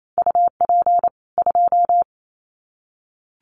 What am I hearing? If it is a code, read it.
Morse code, heard as a keyed tone, about 28 wpm, 700 Hz. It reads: UP 2
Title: Common QSO Phrases 28wpm